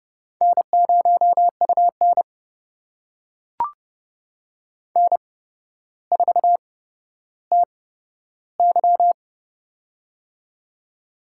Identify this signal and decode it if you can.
Morse code, heard as a keyed tone, about 31 wpm, spaced as for 13 wpm, 700 Hz. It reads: N0UN N4TY
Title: USA Call Signs - 4 Characters - 8x Character Spacing - 30wpm